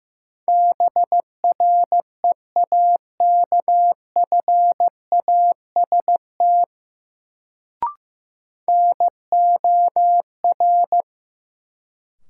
Morse code, heard as a keyed tone, about 15 wpm, 700 Hz. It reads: BREAKFAST NOR
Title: Sentences from Top 1500 Words - New Word Drill - 15wpm